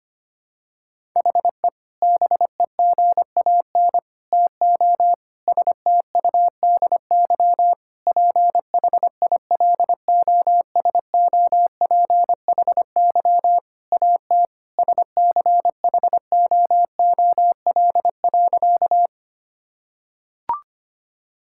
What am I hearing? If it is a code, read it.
Morse code, heard as a keyed tone, about 25 wpm, 700 Hz. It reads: HE BEGAN TO STUDY PHILOSOPHY AT SCHOOL.
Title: Sentences from Top 1000 Words - Review Part 6 - 25wpm